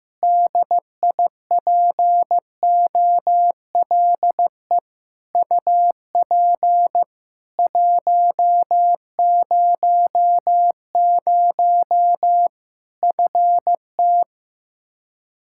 Morse code, heard as a keyed tone, about 16 wpm, 700 Hz. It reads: DIPOLE UP 100 FT
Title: Practice QSO - Exchange 2 - Info - 15wpm